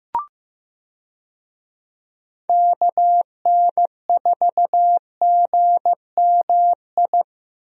Morse code, heard as a keyed tone, about 16 wpm, 700 Hz. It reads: KN4GMI
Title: USA Call Signs - 6 Characters - Repeated 3x - 15wpm